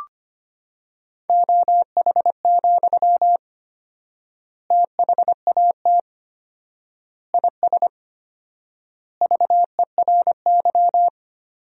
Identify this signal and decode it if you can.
Morse code, heard as a keyed tone, about 25 wpm, 700 Hz. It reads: OH, THAT IS VERY
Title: 4x Word-Spacing - Sentences from Top 500 Words - 25wpm